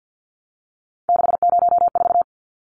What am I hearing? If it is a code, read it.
Morse code, heard as a keyed tone, about 50 wpm, 700 Hz. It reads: 604
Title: Numbers - 3-Digit 50wpm